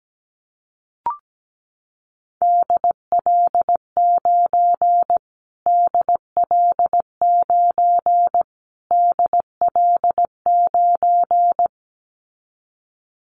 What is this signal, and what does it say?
Morse code, heard as a keyed tone, about 17 wpm, 700 Hz. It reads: DL9 DL9 DL9
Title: Callsign Prefixes for Top 25 DXCC - Repeated 3x - 17wpm